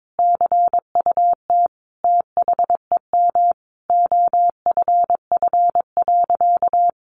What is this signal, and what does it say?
Morse code, heard as a keyed tone, about 22 wpm, 700 Hz. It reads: CUT THEM OFF.
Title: Sentences from Top 1000 Words 22wpm